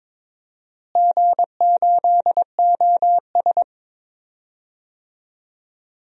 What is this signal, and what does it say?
Morse code, heard as a keyed tone, about 22 wpm, 700 Hz. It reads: G8OS
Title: DXCC-100 Call Signs - 4 Characters - Repeated 2x - 22wpm